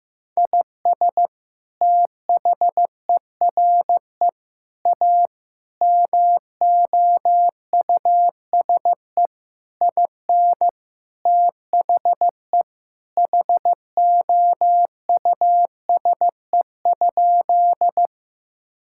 Morse code, heard as a keyed tone, about 16 wpm, 700 Hz. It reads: IS THERE A MOUSE IN THE HOUSE?
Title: Sentences from Top 1000 Words - Review Part 5 - 15wpm